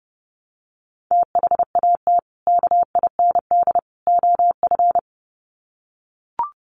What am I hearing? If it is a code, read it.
Morse code, heard as a keyed tone, about 30 wpm, 700 Hz. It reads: THAT KIND OF
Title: Sets of 3 Words 30wpm